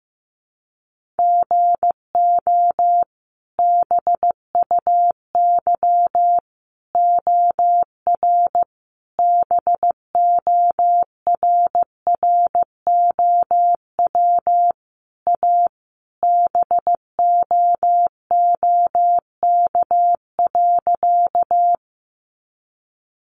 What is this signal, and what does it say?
Morse code, heard as a keyed tone, about 15 wpm, 700 Hz. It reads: GO BUY OR BORROW A BOOK.
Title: Sentences from Top 1500 Words 15wpm